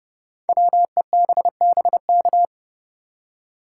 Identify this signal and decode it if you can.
Morse code, heard as a keyed tone, about 31 wpm, 700 Hz. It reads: WEBBK
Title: Code Groups - 5 Letter:Numbers - 30wpm